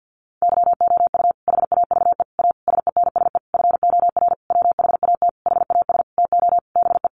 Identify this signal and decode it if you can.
Morse code, heard as a keyed tone, about 50 wpm, 700 Hz. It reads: YOU HAVE A SENSE FOR WHAT HAS TO BE
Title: Sentences from Top 700 Words 50wpm